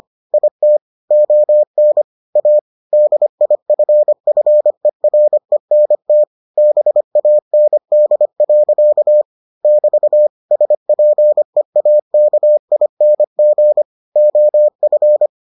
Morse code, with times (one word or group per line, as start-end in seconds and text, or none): 0.3-0.8 IT
1.1-2.0 ON
2.4-2.6 A
2.9-6.2 DIFFERENT
6.6-9.2 BAND.
9.6-10.3 <BT>
10.5-13.8 SPEAKING
14.2-15.3 OF